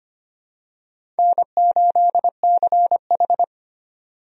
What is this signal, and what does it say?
Morse code, heard as a keyed tone, about 25 wpm, 700 Hz. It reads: N8CH
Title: USA Call Signs - 4 Characters - 25wpm